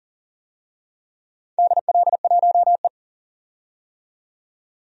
Morse code, heard as a keyed tone, about 40 wpm, 700 Hz. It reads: DL1E
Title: DXCC-100 Call Signs - 4 Characters - 40wpm